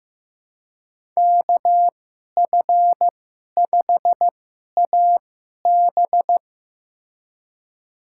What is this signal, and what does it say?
Morse code, heard as a keyed tone, about 15 wpm, spaced as for 11 wpm, 700 Hz. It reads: KF5AB
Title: USA Call Signs - 2x Character Spacing - 15wpm